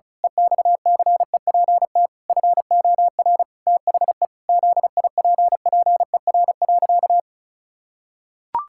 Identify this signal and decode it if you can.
Morse code, heard as a keyed tone, about 34 wpm, 700 Hz. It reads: EXCEPT FOR THE ZIPPER.
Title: Pangram Challenge v2 - 35wpm